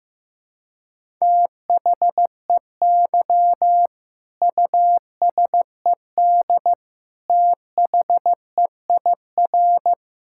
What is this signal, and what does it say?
Morse code, heard as a keyed tone, about 15 wpm, 700 Hz. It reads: THEY USED THEIR
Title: Sentences from Top 1000 Words - Review Part 6 - 15wpm